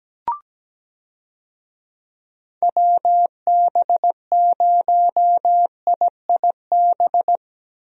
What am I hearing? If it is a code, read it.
Morse code, heard as a keyed tone, about 17 wpm, 700 Hz. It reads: WB0IIB
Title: USA Call Signs - 6 Characters - Repeated 3x - 17wpm